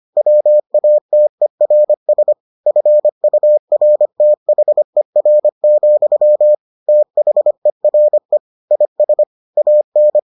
Morse code, heard as a keyed tone, about 25 wpm, 600 Hz. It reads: WATERS FURTHER, THERE IS AN